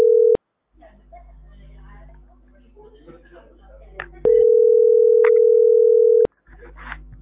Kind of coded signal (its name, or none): telephone call progress tone